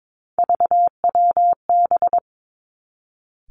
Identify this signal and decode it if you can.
Morse code, heard as a keyed tone, about 22 wpm, 700 Hz. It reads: VWB
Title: Three Challenging Letter-Number Combinations 22wpm